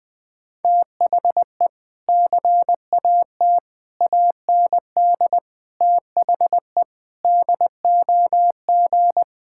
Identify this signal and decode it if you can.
Morse code, heard as a keyed tone, about 21 wpm, 700 Hz. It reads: THE CAT AND THE DOG
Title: Sentences with 2-Word Phrases - Spoken Beginning - 20wpm